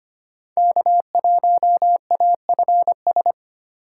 Morse code, heard as a keyed tone, about 25 wpm, 700 Hz. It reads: K1AFS
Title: USA Call Signs - 5 Characters - 3x Character Spacing - 25wpm